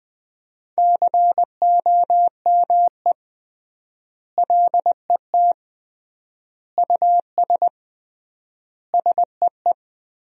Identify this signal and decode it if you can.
Morse code, heard as a keyed tone, about 20 wpm, 700 Hz. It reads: COME LET US SEE
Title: 3x Word-Spacing - Sentences from Top 300 Words - 20wpm